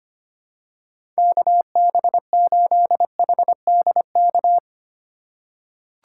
Morse code, heard as a keyed tone, about 25 wpm, 700 Hz. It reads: KB8HDK
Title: USA Call Signs - 6 Characters - 2x Character Spacing - 25wpm